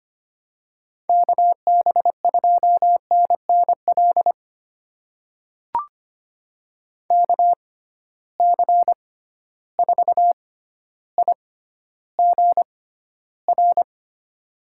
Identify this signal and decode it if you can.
Morse code, heard as a keyed tone, about 25 wpm, spaced as for 13 wpm, 700 Hz. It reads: KB2NNL KC4IGR
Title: USA Call Signs - 6 Characters - 6x Character Spacing - 25wpm